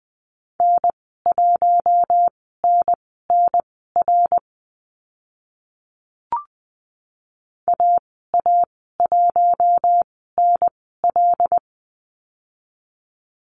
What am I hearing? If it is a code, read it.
Morse code, heard as a keyed tone, about 20 wpm, spaced as for 15 wpm, 700 Hz. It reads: N1NNR AA1NL
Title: 076 Lesson 13g - Letter L - Call Signs 20wpm